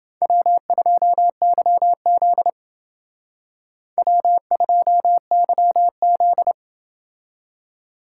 Morse code, heard as a keyed tone, about 28 wpm, 700 Hz. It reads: W2YZ W2YZ